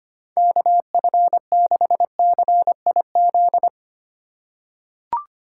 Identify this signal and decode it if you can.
Morse code, heard as a keyed tone, about 25 wpm, 700 Hz. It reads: KF6CIZ
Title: USA Call Signs - 6 Characters - 3x Character Spacing - 25wpm